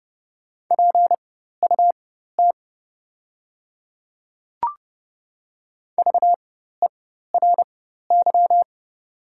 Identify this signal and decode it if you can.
Morse code, heard as a keyed tone, about 30 wpm, spaced as for 14 wpm, 700 Hz. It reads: PUT VERY
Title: 4x Character Spacing - Top 200 Words - 30wpm